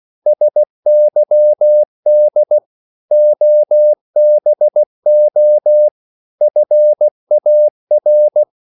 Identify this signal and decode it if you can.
Morse code, heard as a keyed tone, about 16 wpm, 600 Hz. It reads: SYD OBO FAR
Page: Korte danske ord 16wpm